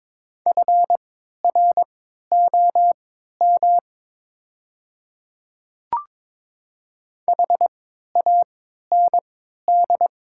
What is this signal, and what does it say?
Morse code, heard as a keyed tone, about 22 wpm, spaced as for 12 wpm, 700 Hz. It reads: FROM HAND
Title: 3x Character Spacing - Top 200 Words - 22wpm